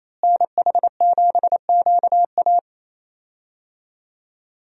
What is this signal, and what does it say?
Morse code, heard as a keyed tone, about 28 wpm, 700 Hz. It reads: NH7QA
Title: USA Call Signs - 5 Characters - Repeated 2x - 28wpm